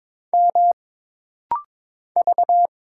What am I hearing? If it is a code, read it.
Morse code, heard as a keyed tone, about 22 wpm, 700 Hz. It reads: M V
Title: Single Letter-Number - Rapid-Fire 22wpm